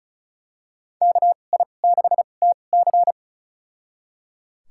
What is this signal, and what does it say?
Morse code, heard as a keyed tone, about 34 wpm, spaced as for 25 wpm, 700 Hz. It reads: KI6TC
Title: USA Call Signs - 5 Characters - 2x Character Spacing - 35wpm